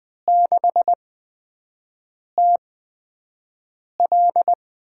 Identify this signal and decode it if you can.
Morse code, heard as a keyed tone, about 20 wpm, spaced as for 5 wpm, 700 Hz. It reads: 6TL
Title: USA Call Signs - 8x Character Spacing - 20wpm